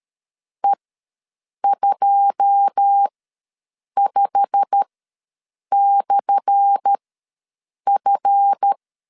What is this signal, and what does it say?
Morse code, heard as a keyed tone, about 13 wpm, spaced as for 7 wpm, 800 Hz. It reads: E25/F
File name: LEZIONI / 10 SIGNS / 10A SLASH ws 12wpm 3443